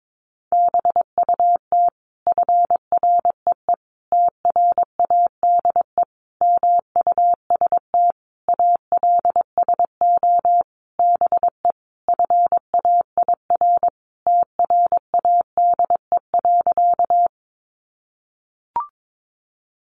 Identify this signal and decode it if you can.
Morse code, heard as a keyed tone, about 22 wpm, 700 Hz. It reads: BUT FREE TRADE MUST ALSO BE FAIR TRADE.
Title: Sentences from Top 1300 Words 22wpm